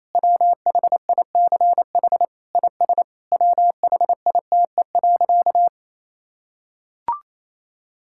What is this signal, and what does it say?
Morse code, heard as a keyed tone, about 27 wpm, 700 Hz. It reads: WHICH IS WHITE.
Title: Sentences from Top 1600 Words 28wpm